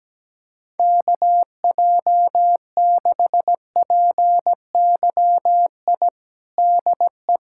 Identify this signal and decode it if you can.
Morse code, heard as a keyed tone, about 17 wpm, 700 Hz. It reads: KJ6PYI DE
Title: USA Callsign DE Callsign - 6 Characters - 17wpm